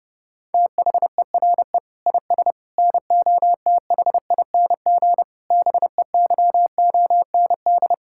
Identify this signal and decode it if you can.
Morse code, heard as a keyed tone, about 32 wpm, 700 Hz. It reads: THERE IS NOTHING BEYOND
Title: Sentences from Top 1000 Words - Review Part 6 - 30wpm